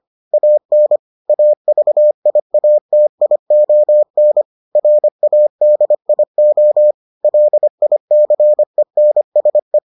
Morse code, heard as a keyed tone, about 25 wpm, 600 Hz. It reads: AN AVIATION RADIO LICENSE